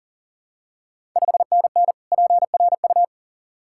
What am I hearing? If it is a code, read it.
Morse code, heard as a keyed tone, about 43 wpm, 690 Hz. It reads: 5NN PRU
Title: Ontario QSO Party Exchange 40wpm